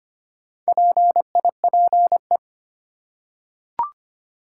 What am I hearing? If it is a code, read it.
Morse code, heard as a keyed tone, about 25 wpm, 700 Hz. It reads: PIPE
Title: Four Letter Words - 25wpm